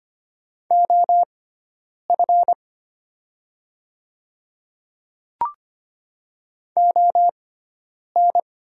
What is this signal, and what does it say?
Morse code, heard as a keyed tone, about 25 wpm, spaced as for 9 wpm, 700 Hz. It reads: OF ON
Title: 6x Character Spacing - Top 100 Words - 25wpm